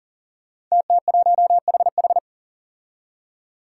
Morse code, heard as a keyed tone, about 38 wpm, 700 Hz. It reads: TT1HH
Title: Five Challenging Letter-Number Combinations 40wpm